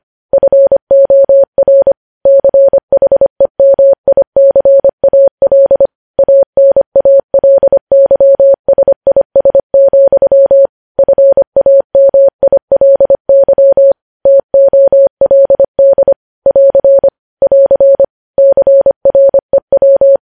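Morse code, heard as a keyed tone, about 25 wpm, 550 Hz. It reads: FOR CHEMICAL ANALYSIS, FAMILY TOLD <AR> <AR> CREW
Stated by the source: Morse news headlines at 25 WPM.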